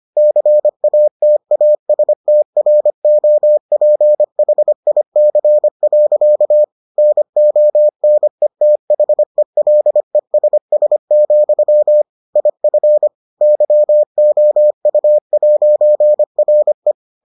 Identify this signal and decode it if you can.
Morse code, heard as a keyed tone, about 25 wpm, 600 Hz. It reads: CATASTROPHIC. NONETHELESS, IF YOU'RE